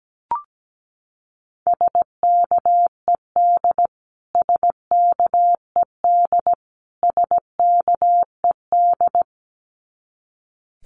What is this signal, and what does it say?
Morse code, heard as a keyed tone, about 17 wpm, 700 Hz. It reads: SKED SKED SKED
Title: Abbreviations Repeated 3x - 17wpm